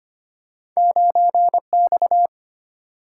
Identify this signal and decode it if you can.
Morse code, heard as a keyed tone, about 25 wpm, 700 Hz. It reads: 9X
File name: Two Letter-Number Combinations - 25wpm